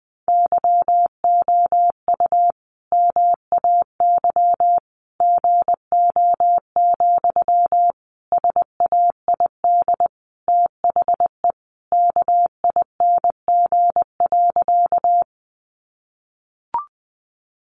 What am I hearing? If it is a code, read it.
Morse code, heard as a keyed tone, about 19 wpm, 700 Hz. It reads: YOU MAY GO, SAID THE KING.
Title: Sentences from Top 400 Words 20wpm